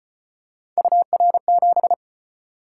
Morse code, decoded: UR7

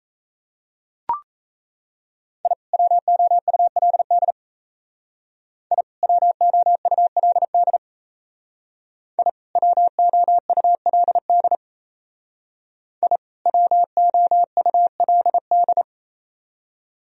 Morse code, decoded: I WOULD I WOULD I WOULD I WOULD

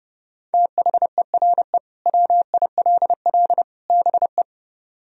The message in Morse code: THERE WILL BE

